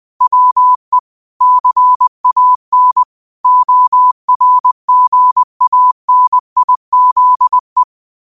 Morse code, heard as a keyed tone, about 20 wpm, 1000 Hz. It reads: WE CAN ORGANIZE
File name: qotdfiles / 20wpm